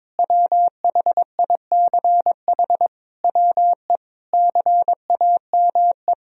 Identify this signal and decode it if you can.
Morse code, heard as a keyed tone, about 21 wpm, 700 Hz. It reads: WHICH WE CAME
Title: Sentences from Top 1000 Words - Review Part 4 - 22wpm